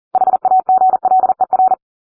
Morse code, heard as a keyed tone, about 40 wpm, 750 Hz. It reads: HAGLER